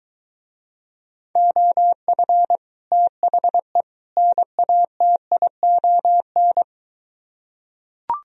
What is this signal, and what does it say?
Morse code, heard as a keyed tone, about 23 wpm, 700 Hz. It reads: OF THE NATION